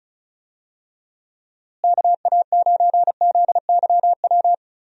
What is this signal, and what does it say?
Morse code, heard as a keyed tone, about 34 wpm, 700 Hz. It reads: KA9ZYW
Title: USA Call Signs - 6 Characters - Repeated 3x - 35wpm